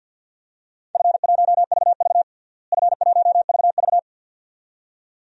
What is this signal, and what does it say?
Morse code, heard as a keyed tone, about 50 wpm, 700 Hz. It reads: F1VV F1VV
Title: DXCC-100 Call Signs - 4 Characters - Repeated 2x - 50wpm